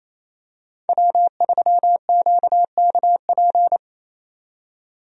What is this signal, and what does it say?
Morse code, heard as a keyed tone, about 27 wpm, 700 Hz. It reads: W3QKP
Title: USA Call Signs - 5 Characters - 28wpm